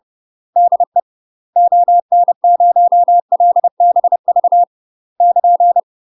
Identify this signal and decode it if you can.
Morse code, heard as a keyed tone, about 30 wpm, 700 Hz. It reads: DE ON0LBV <KN>
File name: UK MASTER QSOs TRAINING / 30WPM / UK MASTER QSOs 0013